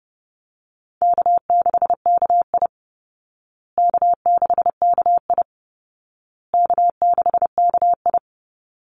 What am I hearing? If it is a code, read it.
Morse code, heard as a keyed tone, about 30 wpm, 700 Hz. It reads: K6KI K6KI K6KI
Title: Call Signs Repeated 2x - 30wpm